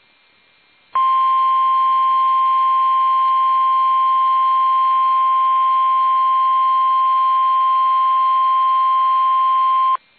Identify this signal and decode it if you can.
Emergency Alert System (SAME) tones (the 1050 Hz attention tone). It is heard from 0.9 to 10.0 s.
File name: eas / KUSO / KUSO